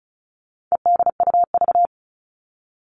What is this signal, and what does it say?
Morse code, heard as a keyed tone, about 34 wpm, 700 Hz. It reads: EDUV
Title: All Cut Numbers - 4-Digit 35wpm